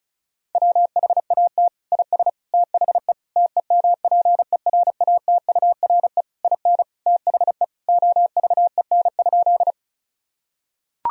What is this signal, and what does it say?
Morse code, heard as a keyed tone, about 36 wpm, 700 Hz. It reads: WHAT IS THE TEMPERATURE IN THE OVEN?
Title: Sentences from Top 1000 Words - Review Part 4 - 35wpm